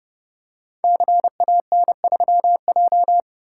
Morse code, heard as a keyed tone, about 29 wpm, 700 Hz. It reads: CAN3J
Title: Code Groups - 5 Letter:Numbers - 30wpm